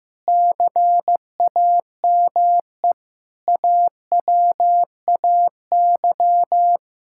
Morse code, heard as a keyed tone, about 16 wpm, 700 Hz. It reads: CAME AWAY
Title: Sentences from Top 1000 Words - Review Part 6 - 15wpm